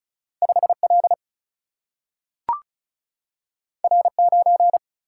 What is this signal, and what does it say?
Morse code, heard as a keyed tone, about 36 wpm, 700 Hz. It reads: 5 L R 9
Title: Single-Letter Single-Number 35wpm